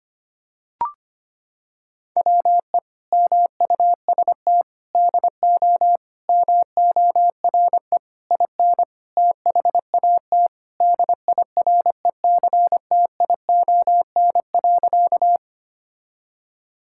Morse code, heard as a keyed tone, about 25 wpm, 700 Hz. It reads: WE MUST DO MORE IN THAT DIRECTION.
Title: Sentences from Top 1000 Words - Review Part 5 - 25wpm